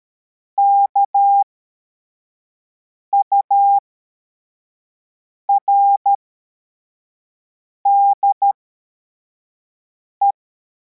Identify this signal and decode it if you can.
Morse code, heard as a keyed tone, about 13 wpm, spaced as for 4 wpm, 800 Hz. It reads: KURDE
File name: formation / cw / Text42